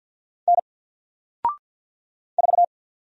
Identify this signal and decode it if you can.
Morse code, heard as a keyed tone, about 50 wpm, 700 Hz. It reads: N 4
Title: Single Letter-Number 50wpm